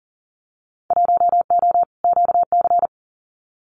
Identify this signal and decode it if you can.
Morse code, heard as a keyed tone, about 40 wpm, 700 Hz. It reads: 1O QC
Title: Winter Field Day Exchanges 40wpm